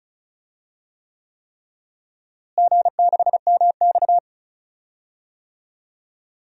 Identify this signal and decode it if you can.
Morse code, heard as a keyed tone, about 36 wpm, 700 Hz. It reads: G6MX
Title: DXCC-100 Call Signs - 4 Characters - Repeated 3x - 35wpm